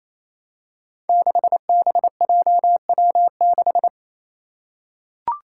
Morse code, heard as a keyed tone, about 28 wpm, 700 Hz. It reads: 6BJW6
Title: Five Challenging Letter-Number Combinations 28wpm